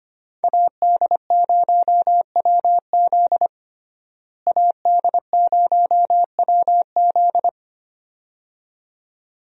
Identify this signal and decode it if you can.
Morse code, heard as a keyed tone, about 25 wpm, 700 Hz. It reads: AD0WZ AD0WZ
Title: USA Call Signs - 5 Characters - Repeated 2x - 25wpm